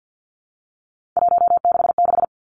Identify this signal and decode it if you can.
Morse code, heard as a keyed tone, about 50 wpm, 700 Hz. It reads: 166
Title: Three Challenging Letter-Number Combinations 50wpm